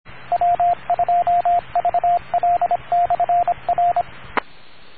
Morse code, heard as a keyed tone, about 26 wpm, 700 Hz, with radio noise. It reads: W2VL/R